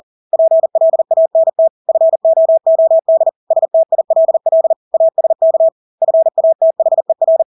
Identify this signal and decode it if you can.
Morse code, heard as a keyed tone, about 40 wpm, 650 Hz. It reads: PLANT FOOD STILL ASK FATHER
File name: Engelske ord 40-40 wpm